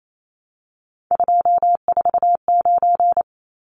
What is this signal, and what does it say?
Morse code, heard as a keyed tone, about 28 wpm, 700 Hz. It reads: 249
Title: Numbers - 3-Digit 28wpm